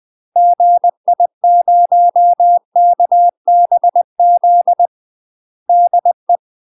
Morse code, heard as a keyed tone, about 20 wpm, 700 Hz. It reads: GI0KBZ DE
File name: UK MASTER QSOs TRAINING / 20WPM / UK MASTER QSOs 0037